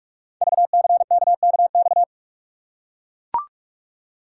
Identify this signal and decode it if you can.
Morse code, heard as a keyed tone, about 44 wpm, 700 Hz. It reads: VCKKX
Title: Code Groups - 5 Letters - 45wpm